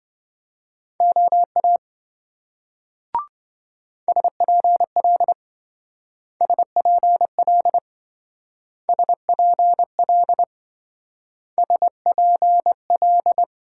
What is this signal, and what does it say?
Morse code, heard as a keyed tone, about 27 wpm, 700 Hz. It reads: OA SPL SPL SPL SPL